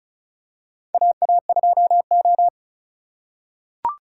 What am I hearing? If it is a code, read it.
Morse code, heard as a keyed tone, about 33 wpm, 700 Hz. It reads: AA2O
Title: USA Call Signs - 4x Character Spacing - 35wpm